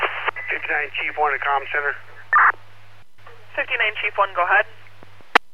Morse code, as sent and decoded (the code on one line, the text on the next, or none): none